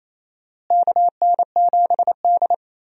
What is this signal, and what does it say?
Morse code, heard as a keyed tone, about 28 wpm, 700 Hz. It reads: KN7D
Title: USA Call Signs - 4 Characters - Repeated 2x - 28wpm